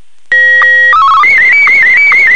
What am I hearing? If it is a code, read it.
SSTV (slow-scan television), Martin 1.